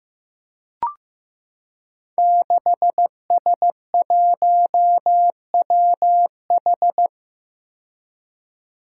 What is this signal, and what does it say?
Morse code, heard as a keyed tone, about 15 wpm, 700 Hz. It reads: 6S1WH
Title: Five Challenging Letter-Number Combinations 15wpm